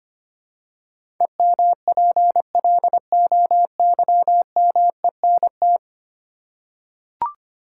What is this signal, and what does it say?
Morse code, heard as a keyed tone, about 25 wpm, 700 Hz. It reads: EMPLOYMENT
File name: Ten Letter Words - 25wpm